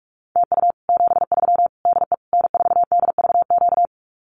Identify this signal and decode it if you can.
Morse code, heard as a keyed tone, about 44 wpm, 700 Hz. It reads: TU 73 DE N4DVQ
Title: Common QSO Phrases Encore 45wpm